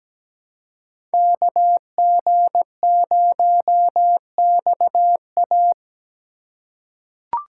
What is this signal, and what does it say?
Morse code, heard as a keyed tone, about 17 wpm, 700 Hz. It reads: KG0XA
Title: USA Call Signs - 5 Characters - 4x Character Spacing - 17wpm